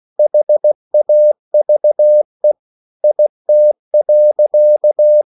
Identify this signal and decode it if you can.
Morse code, heard as a keyed tone, about 16 wpm, 600 Hz. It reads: HAVE IT.